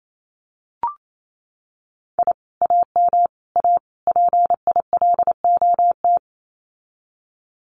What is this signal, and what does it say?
Morse code, heard as a keyed tone, about 28 wpm, 700 Hz. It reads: I AM A PILOT
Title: Common QSO Phrases Encore 28wpm